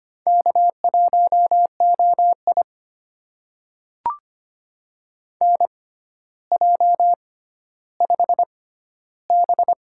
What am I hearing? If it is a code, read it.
Morse code, heard as a keyed tone, about 25 wpm, spaced as for 13 wpm, 700 Hz. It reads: K1OI NJ5B
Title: USA Call Signs - 4 Characters - 6x Character Spacing - 25wpm